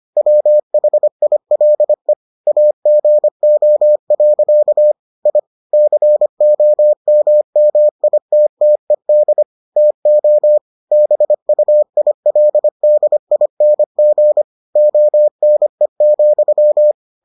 Morse code, heard as a keyed tone, about 25 wpm, 600 Hz. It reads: WHILE AGO. I COMMITTED TO BUILDING ONE,